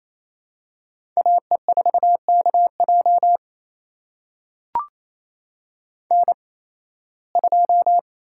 Morse code, heard as a keyed tone, about 28 wpm, spaced as for 18 wpm, 700 Hz. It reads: AE4KJ N2
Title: USA Call Signs - 5 Characters - 8x Character Spacing - 28wpm